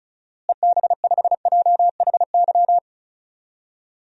Morse code, heard as a keyed tone, about 36 wpm, 700 Hz. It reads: EB5JHY
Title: DXCC-100 Call Signs - 6 Characters - Repeated 3x - 35wpm